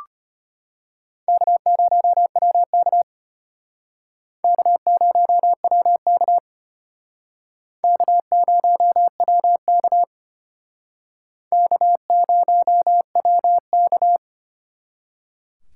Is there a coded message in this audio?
Yes: Morse code from 1.3 to 14.2 s, heard as a keyed tone, about 32 wpm, 700 Hz. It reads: K0WK K0WK K0WK K0WK